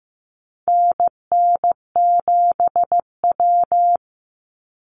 Morse code, heard as a keyed tone, about 15 wpm, 700 Hz. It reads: NN7W
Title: Call Signs Repeated 3x - 15wpm